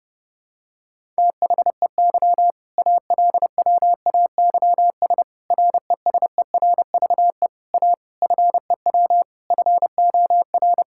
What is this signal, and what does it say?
Morse code, heard as a keyed tone, about 30 wpm, 700 Hz. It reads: THEY ALWAYS RESERVE A FEW FOR
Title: Sentences from Top 1000 Words - Review Part 6 - 30wpm